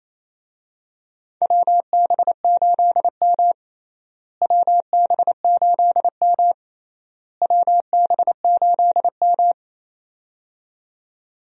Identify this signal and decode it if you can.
Morse code, heard as a keyed tone, about 28 wpm, 700 Hz. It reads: WB8M WB8M WB8M
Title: USA Call Signs - 4 Characters - Repeated 3x - 28wpm